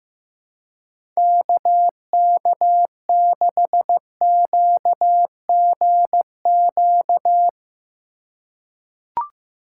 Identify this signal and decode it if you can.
Morse code, heard as a keyed tone, about 16 wpm, 700 Hz. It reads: KK6QGQ
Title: USA Call Signs - 6 Characters - 4x Character Spacing - 15wpm